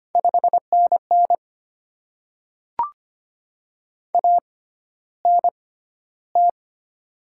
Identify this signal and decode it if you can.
Morse code, heard as a keyed tone, about 25 wpm, spaced as for 13 wpm, 700 Hz. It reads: 5NN ANT
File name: Top 20 QSO Elements - 6x Character Spacing - 25wpm